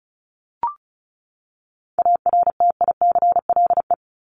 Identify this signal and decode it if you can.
Morse code, heard as a keyed tone, about 34 wpm, 700 Hz. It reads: ARTICLE
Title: Top 751-1000 Words 35wpm